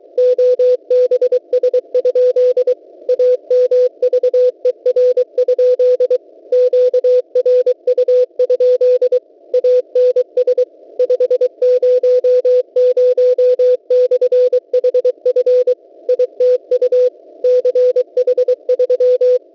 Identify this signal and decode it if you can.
Morse code, heard as a keyed tone, about 23 wpm, 500 Hz. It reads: OBS? AMVER? QRU? ANS 500/HF ITU CH3
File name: w7pua-2.ddns.net 2024-01-20T21 20 08Z 6477.50 cw